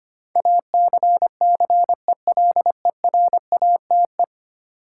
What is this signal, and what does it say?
Morse code, heard as a keyed tone, about 25 wpm, 700 Hz. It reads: ACCELERATE
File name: 10 Letter Words - First 3 Letters Given - 25wpm